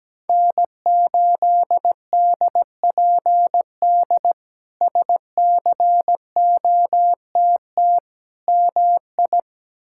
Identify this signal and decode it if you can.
Morse code, heard as a keyed tone, about 17 wpm, 700 Hz. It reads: N8DPD SCOTT MI
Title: K1USN SST QSO Practice - 17wpm